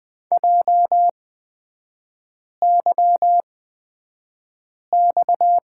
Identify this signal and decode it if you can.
Morse code, heard as a keyed tone, about 20 wpm, spaced as for 5 wpm, 690 Hz. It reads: JYX
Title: 229 Lesson 37e - Letter x - Three Characters 20wpm